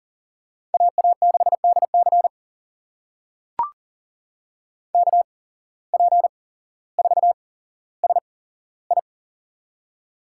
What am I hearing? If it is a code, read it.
Morse code, heard as a keyed tone, about 39 wpm, spaced as for 17 wpm, 700 Hz. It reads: AA6DC KP4SI
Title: USA Call Signs - 8x Character Spacing - 40wpm